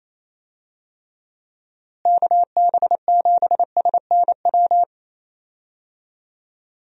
Morse code, heard as a keyed tone, about 28 wpm, 700 Hz. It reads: KB7SNW
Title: USA Call Signs - 6 Characters - Repeated 3x - 28wpm